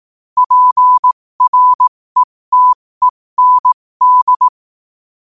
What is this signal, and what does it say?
Morse code, heard as a keyed tone, about 18 wpm, spaced as for 15 wpm, 1000 Hz. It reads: PRETEND
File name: qotdfiles / 15wpm